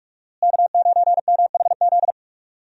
Morse code, heard as a keyed tone, about 46 wpm, 700 Hz. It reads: K9MHZ
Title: USA Call Signs - 5 Characters - Repeated 2x - 45wpm